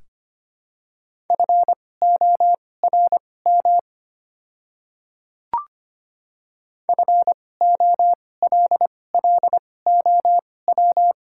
Morse code, heard as a keyed tone, about 25 wpm, spaced as for 18 wpm, 690 Hz. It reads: FORM FOLLOW
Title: 2x Character Spacing - Top 200 Words - 25wpm